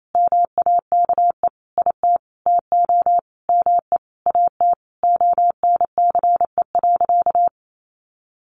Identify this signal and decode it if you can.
Morse code, heard as a keyed tone, about 28 wpm, 700 Hz. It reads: MAKE IT TO ME AT ONCE.
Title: Sentences from Top 500 Words 28wpm